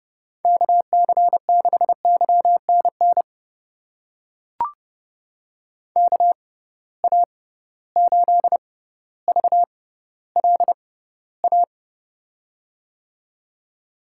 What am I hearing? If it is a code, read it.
Morse code, heard as a keyed tone, about 31 wpm, spaced as for 16 wpm, 700 Hz. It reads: KC6YNN KA8VLA
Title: USA Call Signs - 6 Characters - 6x Character Spacing - 30wpm